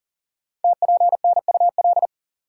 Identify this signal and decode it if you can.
Morse code, heard as a keyed tone, about 42 wpm, 690 Hz. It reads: TPNUL
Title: Code Groups - 5 Letters - 40wpm